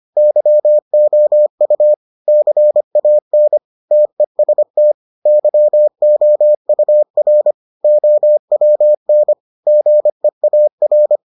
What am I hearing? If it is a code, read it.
Morse code, heard as a keyed tone, about 25 wpm, 600 Hz. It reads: YOU CAN TEST YOUR OWN GEAR